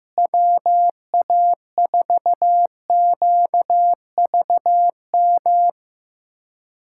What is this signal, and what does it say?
Morse code, heard as a keyed tone, about 15 wpm, 700 Hz. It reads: WA4QVM
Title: USA Callsign DE Callsign - 6 Characters - 15wpm